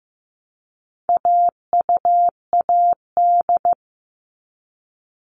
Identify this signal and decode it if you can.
Morse code, heard as a keyed tone, about 15 wpm, 700 Hz. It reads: AUAD
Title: All Cut Numbers - 4-Digit 15wpm